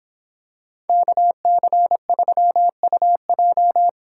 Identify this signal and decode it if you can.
Morse code, heard as a keyed tone, about 26 wpm, 700 Hz. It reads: KC3UJ